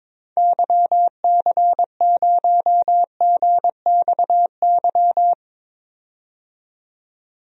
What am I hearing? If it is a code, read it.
Morse code, heard as a keyed tone, about 22 wpm, 700 Hz. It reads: YC0GXY
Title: DXCC-100 Call Signs - 6 Characters - 22wpm